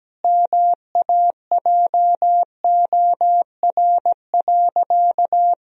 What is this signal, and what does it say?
Morse code, heard as a keyed tone, about 17 wpm, 700 Hz. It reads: MAJOR.
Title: Sentences from Top 1000 Words - Review Part 4 - 17wpm